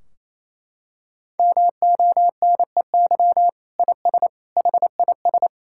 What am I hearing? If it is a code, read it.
Morse code, heard as a keyed tone, about 28 wpm, 700 Hz. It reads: MONEY IS HIS
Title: Sentences from Top 1600 Words 28wpm